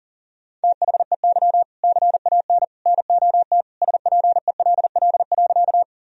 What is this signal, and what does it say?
Morse code, heard as a keyed tone, about 37 wpm, 700 Hz. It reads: THEY CAN NOT SPELL.
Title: Sentences from Top 1000 Words - Review Part 1 - 40wpm